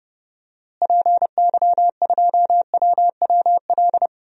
Morse code, heard as a keyed tone, about 29 wpm, 700 Hz. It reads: PY2WWL
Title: DXCC-100 Call Signs - 6 Characters - Repeated 2x - 30wpm